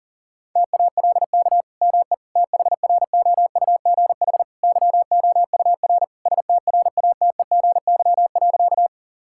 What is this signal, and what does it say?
Morse code, heard as a keyed tone, about 40 wpm, 700 Hz. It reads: TALK ME THROUGH YOUR STRATEGY.
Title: Sentences from Top 1000 Words - Review Part 5 - 40wpm